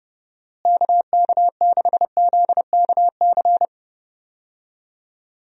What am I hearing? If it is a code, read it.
Morse code, heard as a keyed tone, about 31 wpm, 700 Hz. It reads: KK6ZKC
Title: USA Call Signs - 6 Characters - Repeated 3x - 30wpm